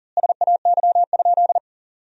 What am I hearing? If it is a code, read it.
Morse code, heard as a keyed tone, about 40 wpm, 700 Hz. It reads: SAY?
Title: 4x Word-Spacing - Sentences from Top 200 Words - 40wpm